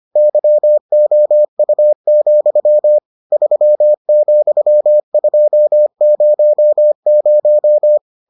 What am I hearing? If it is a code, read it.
Morse code, heard as a keyed tone, about 25 wpm, 600 Hz. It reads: YOU, 3,200